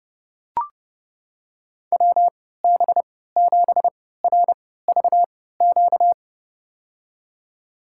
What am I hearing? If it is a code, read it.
Morse code, heard as a keyed tone, about 30 wpm, spaced as for 17 wpm, 700 Hz. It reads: WB7RVQ
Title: USA Call Signs - 6 Characters - 3x Character Spacing - 30wpm